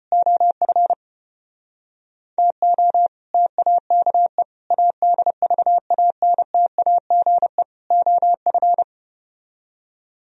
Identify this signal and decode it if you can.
Morse code, heard as a keyed tone, about 33 wpm, 700 Hz. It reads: OF TO TAKE ADVANTAGE OF